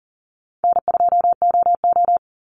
Morse code, heard as a keyed tone, about 38 wpm, 700 Hz. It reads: N2OO
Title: Call Signs - 40wpm